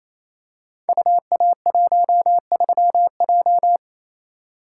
Morse code, heard as a keyed tone, about 28 wpm, 700 Hz. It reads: UA13J
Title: Code Groups - 5 Letter:Numbers - 28wpm